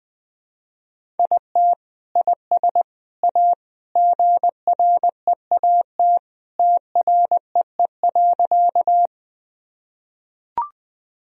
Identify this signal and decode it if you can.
Morse code, heard as a keyed tone, about 20 wpm, 700 Hz. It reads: IT IS A GREAT TREE.
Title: Sentences from Top 1000 Words - Review Part 1 - 20wpm